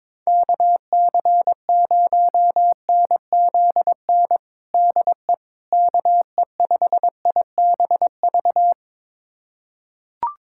Morse code, heard as a keyed tone, about 22 wpm, 700 Hz. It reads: KC0NZN DE KE5IBV
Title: USA Callsign DE Callsign 22wpm